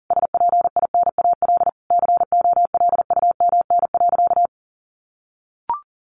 Morse code, heard as a keyed tone, about 40 wpm, 700 Hz. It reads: SPINAL COLUMN.
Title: A Sound Of Thunder 40wpm